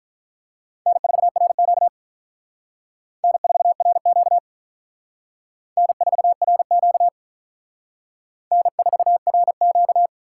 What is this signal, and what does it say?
Morse code, heard as a keyed tone, about 42 wpm, 700 Hz. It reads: N4RQ N4RQ N4RQ N4RQ
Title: Speed Racing - USA 1x2 Call Signs 35wpm